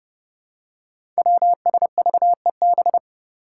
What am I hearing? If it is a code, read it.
Morse code, heard as a keyed tone, about 30 wpm, 700 Hz. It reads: WSVEB
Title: Five Challenging Letter-Number Combinations 30wpm